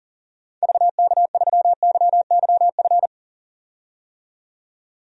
Morse code, heard as a keyed tone, about 40 wpm, 700 Hz. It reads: VK3YYF
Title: DXCC-100 Call Signs - 6 Characters - 40wpm